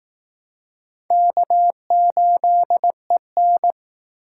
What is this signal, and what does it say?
Morse code, heard as a keyed tone, about 19 wpm, 700 Hz. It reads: K8EN